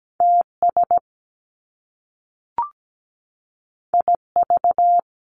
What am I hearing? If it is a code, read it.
Morse code, heard as a keyed tone, about 17 wpm, 700 Hz. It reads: TS IV
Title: Top Two Letter Combinations in Words 17wpm